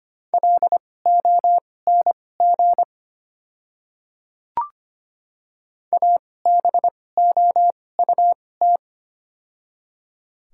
Morse code, heard as a keyed tone, about 25 wpm, spaced as for 18 wpm, 700 Hz. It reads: LONG ABOUT
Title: 2x Character Spacing - Top 100 Words - 25wpm